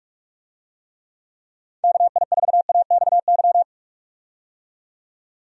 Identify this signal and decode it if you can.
Morse code, heard as a keyed tone, about 43 wpm, 700 Hz. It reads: KI4AXY
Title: USA Call Signs - 6 Characters - 45wpm